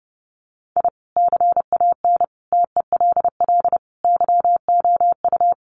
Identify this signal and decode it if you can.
Morse code, heard as a keyed tone, about 30 wpm, 700 Hz. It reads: I CAN TELL YOU
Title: Sets of 5 Words 30wpm